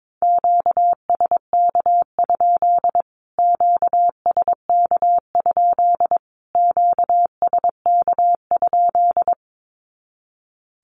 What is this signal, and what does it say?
Morse code, heard as a keyed tone, about 22 wpm, 700 Hz. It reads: QSK? QSK? QSK?
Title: Q-Codes Repeated 3x 22wpm